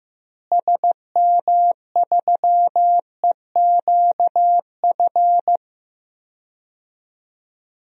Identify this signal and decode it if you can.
Morse code, heard as a keyed tone, about 15 wpm, 700 Hz. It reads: SM3EQF
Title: DXCC-100 Call Signs - 6 Characters - Repeated 3x - 15wpm